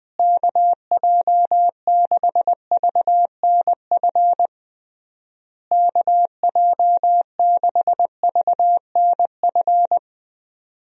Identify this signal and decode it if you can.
Morse code, heard as a keyed tone, about 20 wpm, 700 Hz. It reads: KJ6VNF KJ6VNF
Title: USA Call Signs - 6 Characters - Repeated 3x - 20wpm